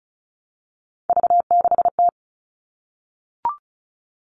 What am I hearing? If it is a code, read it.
Morse code, heard as a keyed tone, about 34 wpm, 700 Hz. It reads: V6T
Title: All Cut Numbers - 3-Digit 35wpm